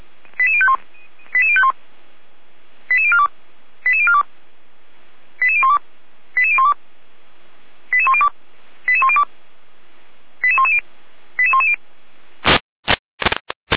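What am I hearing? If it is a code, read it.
Five-tone selective call (ZVEI), heard as quick runs of five tones, tones of about 70 ms. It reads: 88051 88051 88042 88042 88012 88012 88102 88102 88110 88110
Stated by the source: ZVEI five-tone selective call received on an SDR, fire brigade Orlamünde (feuerwehr-orlamuende), Germany.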